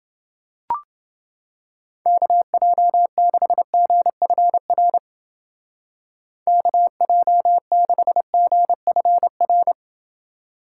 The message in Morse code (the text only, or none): KJ6GFR KJ6GFR